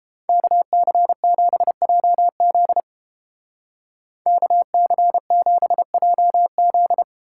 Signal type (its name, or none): Morse code